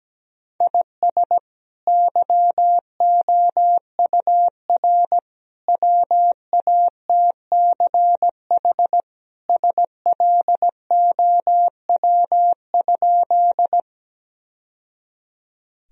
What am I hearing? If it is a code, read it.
Morse code, heard as a keyed tone, about 17 wpm, 700 Hz. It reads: IS YOUR WATCH SLOW?
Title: Sentences from Top 1000 Words - Review Part 2 - 17wpm